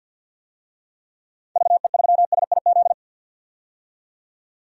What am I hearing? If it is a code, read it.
Morse code, heard as a keyed tone, about 49 wpm, 700 Hz. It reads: VE3SIZ
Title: DXCC-100 Call Signs - 6 Characters - 50wpm